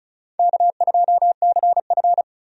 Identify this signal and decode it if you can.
Morse code, heard as a keyed tone, about 36 wpm, 700 Hz. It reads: K2CF
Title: USA Call Signs - 4 Characters - 35wpm